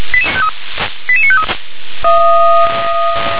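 Five-tone selective call (ZVEI), heard as quick runs of five tones, tones of about 70 ms. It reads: 88953
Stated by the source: ZVEI five-tone selective call received by SDR.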